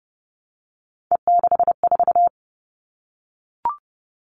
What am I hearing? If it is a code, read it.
Morse code, heard as a keyed tone, about 29 wpm, 700 Hz. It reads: E64
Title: All Cut Numbers - 3-Digit 30wpm